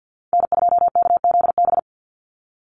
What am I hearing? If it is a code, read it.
Morse code, heard as a keyed tone, about 50 wpm, 700 Hz. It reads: N2KZB
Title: USA Call Signs 50wpm